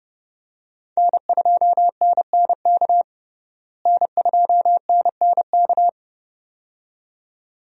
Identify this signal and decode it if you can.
Morse code, heard as a keyed tone, about 30 wpm, 700 Hz. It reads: N2NNK N2NNK
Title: USA Call Signs - 5 Characters - Repeated 2x - 30wpm